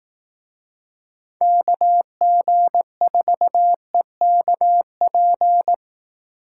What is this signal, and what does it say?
Morse code, heard as a keyed tone, about 18 wpm, 700 Hz. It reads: KG4EKP
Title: Speed Racing - USA 2x3 Call Signs 15wpm